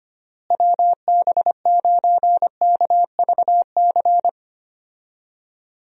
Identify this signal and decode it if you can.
Morse code, heard as a keyed tone, about 25 wpm, 700 Hz. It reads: WB9KVC
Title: USA Call Signs - 6 Characters - Repeated 3x - 25wpm